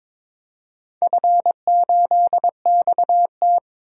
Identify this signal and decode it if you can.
Morse code, heard as a keyed tone, about 22 wpm, 690 Hz. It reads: F8XT
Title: DXCC-100 Call Signs - 4 Characters - 22wpm